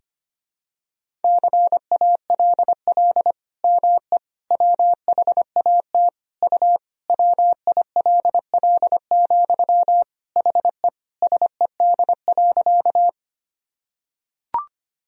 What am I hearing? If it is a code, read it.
Morse code, heard as a keyed tone, about 25 wpm, 700 Hz. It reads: CALL ME WHAT U WILL, HE SED.
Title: Sentences from Top 200 Words and Abbreviations 25wpm